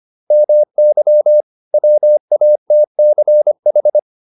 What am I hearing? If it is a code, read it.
Morse code, heard as a keyed tone, about 25 wpm, 600 Hz. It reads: MY WATCH
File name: PR405 25